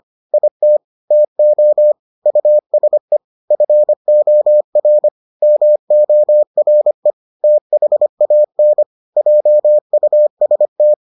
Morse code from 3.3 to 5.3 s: FOR

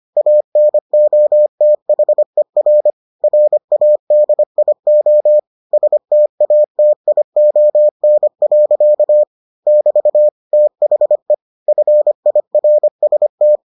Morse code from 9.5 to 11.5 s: <BT> THE

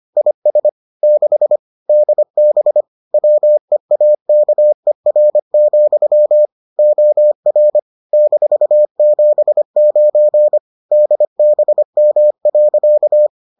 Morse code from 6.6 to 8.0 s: OR

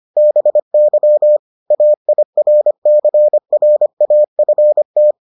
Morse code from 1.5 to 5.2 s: AIRCRAFT

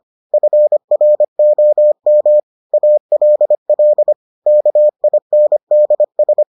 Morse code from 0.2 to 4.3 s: FROM ALL

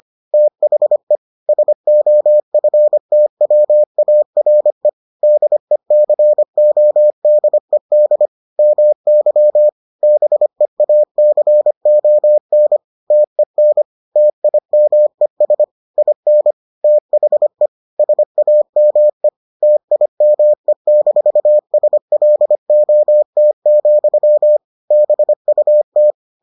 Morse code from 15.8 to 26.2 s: IN THE SAME TIME-SLOT, BUT

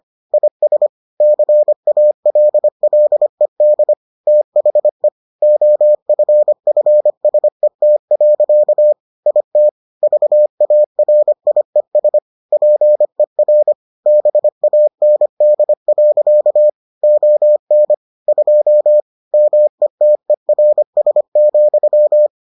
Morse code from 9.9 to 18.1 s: VARIES PER BAND. ON

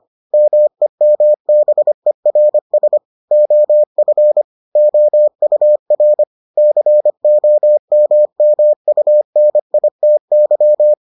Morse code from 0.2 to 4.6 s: MEMBERS OF

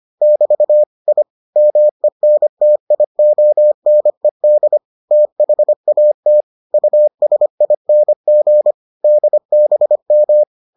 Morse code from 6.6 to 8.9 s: USING

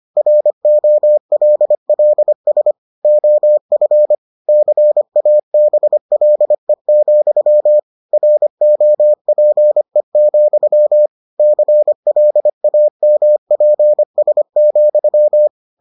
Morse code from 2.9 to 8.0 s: OF CABLE,

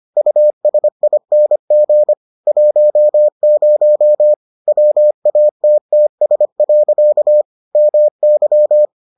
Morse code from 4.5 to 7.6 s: WATTS.

